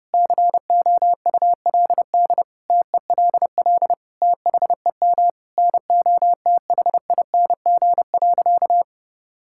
Morse code, heard as a keyed tone, about 30 wpm, 700 Hz. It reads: COULD TELL THEM NOTHING.
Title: Sentences from Top 1000 Words - Review Part 6 - 30wpm